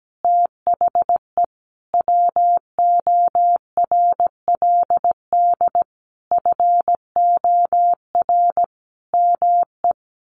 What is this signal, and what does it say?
Morse code, heard as a keyed tone, about 17 wpm, 700 Hz. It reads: THE WORLD FOR ME
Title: Sentences from Top 1500 Words 17wpm